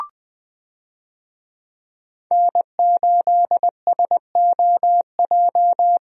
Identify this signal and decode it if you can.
Morse code, heard as a keyed tone, about 20 wpm, 700 Hz. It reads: N8SOJ
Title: USA Call Signs - 5 Characters - Repeated 2x - 20wpm